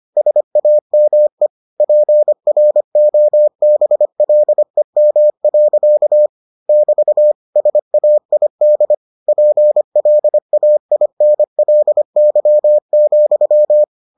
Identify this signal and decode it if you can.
Morse code, heard as a keyed tone, about 25 wpm, 600 Hz. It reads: SAME PROBLEM. <BT> SAID PLAINLY,